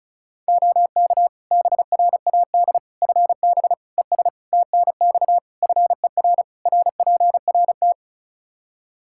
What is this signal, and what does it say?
Morse code, heard as a keyed tone, about 34 wpm, 700 Hz. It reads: OK BRAD FB ES TNX FER RPRT
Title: K1USN SST QSO Practice 35wpm